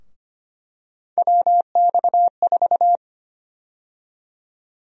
Morse code, heard as a keyed tone, about 25 wpm, 700 Hz. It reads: WX4
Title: USA Call Sign Prefixes - 25wpm